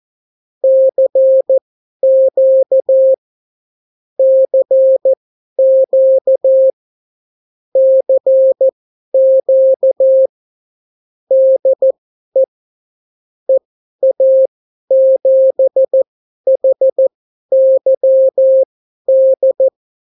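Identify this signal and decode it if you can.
Morse code, heard as a keyed tone, about 14 wpm, spaced as for 11 wpm, 530 Hz. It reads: CQ CQ CQ DE EA7HYD
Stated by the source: Example Spanish CW QSO from a Morse learning book, 14 wpm characters with 8 wpm spacing.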